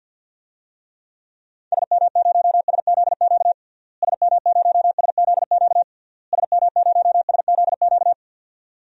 Morse code, heard as a keyed tone, about 45 wpm, 700 Hz. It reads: SM0SZQ SM0SZQ SM0SZQ
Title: DXCC-100 Call Signs - 6 Characters - Repeated 3x - 50wpm